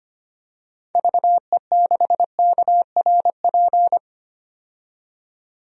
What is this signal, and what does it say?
Morse code, heard as a keyed tone, about 25 wpm, 700 Hz. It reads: VE6KRP
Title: DXCC-100 Call Signs - 6 Characters - 25wpm